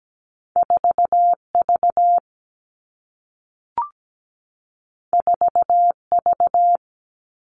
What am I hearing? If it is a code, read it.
Morse code, heard as a keyed tone, about 17 wpm, 700 Hz. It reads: 4V 4V
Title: All Cut Numbers - 2-Digit 17wpm